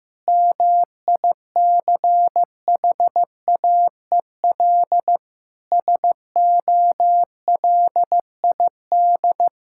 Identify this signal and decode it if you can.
Morse code, heard as a keyed tone, about 15 wpm, 700 Hz. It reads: MICHAEL SOLID
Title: Practice QSO - Exchange 3 - More Info - 15wpm